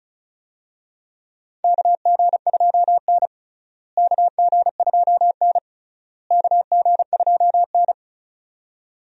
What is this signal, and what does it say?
Morse code, heard as a keyed tone, about 36 wpm, 700 Hz. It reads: KG2N KG2N KG2N
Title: USA Call Signs - 4 Characters - Repeated 3x - 35wpm